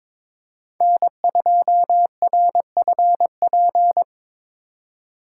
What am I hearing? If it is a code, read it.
Morse code, heard as a keyed tone, about 24 wpm, 700 Hz. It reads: N2RFP
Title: Speed Racing - USA 1x3 Call Signs 22wpm